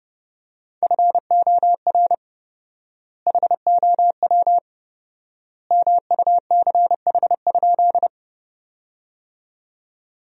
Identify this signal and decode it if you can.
Morse code, heard as a keyed tone, about 30 wpm, 700 Hz. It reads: FOR HOW MUCH?
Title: 4x Word-Spacing - Sentences from Top 200 Words - 30wpm